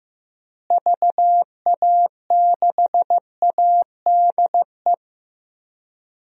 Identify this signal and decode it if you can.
Morse code, heard as a keyed tone, about 16 wpm, 700 Hz. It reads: VA6ADE
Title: DXCC-100 Call Signs - 6 Characters - Repeated 2x - 15wpm